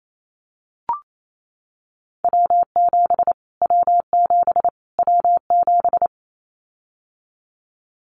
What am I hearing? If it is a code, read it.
Morse code, heard as a keyed tone, about 28 wpm, 700 Hz. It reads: W7 W7 W7
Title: Callsign Prefixes for Top 25 DXCC - Repeated 3x - 28wpm